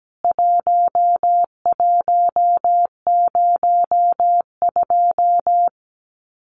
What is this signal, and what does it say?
Morse code, heard as a keyed tone, about 17 wpm, 700 Hz. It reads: 1102
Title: CWT Exchange 17wpm